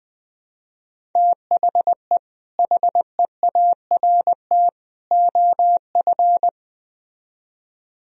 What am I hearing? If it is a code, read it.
Morse code, heard as a keyed tone, about 20 wpm, 700 Hz. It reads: THE HEART OF
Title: Speed-Racing - Sets of 3 Words 17wpm